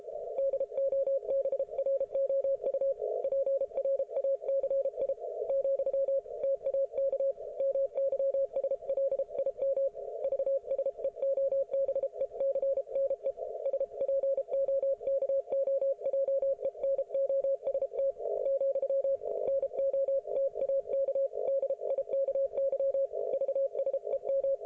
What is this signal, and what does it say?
Morse code, heard as a keyed tone, about 33 wpm, 550 Hz. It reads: DOBROU PRACI , TAK MYSLIM VSEOBECNE SPOKOJENOST , NO TAK DIKY VSEM